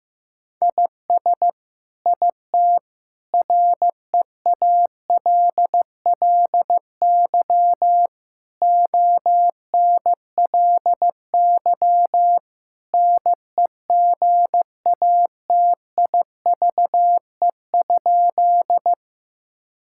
Morse code, heard as a keyed tone, about 16 wpm, 700 Hz. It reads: IS IT REALLY ONLY NEGATIVE?
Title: Sentences from Top 1000 Words - Review Part 6 - 15wpm